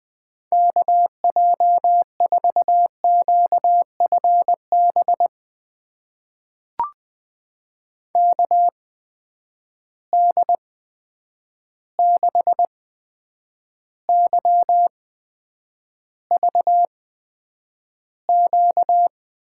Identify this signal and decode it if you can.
Morse code, heard as a keyed tone, about 20 wpm, spaced as for 9 wpm, 700 Hz. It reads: KJ4QFB KD6YVQ
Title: USA Call Signs - 6 Characters - 8x Character Spacing - 20wpm